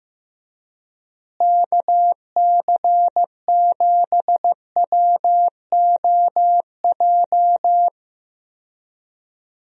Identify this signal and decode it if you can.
Morse code, heard as a keyed tone, about 15 wpm, 700 Hz. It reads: KC7WOJ
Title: USA Call Signs - 6 Characters - Repeated 3x - 15wpm